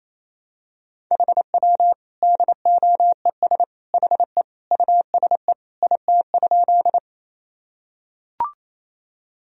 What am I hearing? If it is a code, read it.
Morse code, heard as a keyed tone, about 28 wpm, 700 Hz. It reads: HW DOES HE USE IT?
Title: Sentences from Top 200 Words and Abbreviations 28wpm